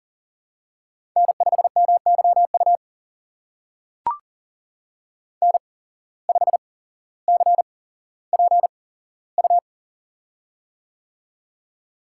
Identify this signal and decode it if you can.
Morse code, heard as a keyed tone, about 43 wpm, spaced as for 17 wpm, 700 Hz. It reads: N5MYU N5CPU
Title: USA Call Signs - 5 Characters - 8x Character Spacing - 40wpm